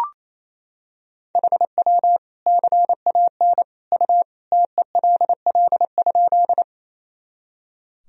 Morse code, heard as a keyed tone, about 28 wpm, 700 Hz. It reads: HW CAN U TELL?
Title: Sentences from Top 200 Words and Abbreviations 28wpm